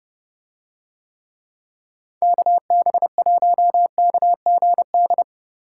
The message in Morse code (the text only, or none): KB1KGD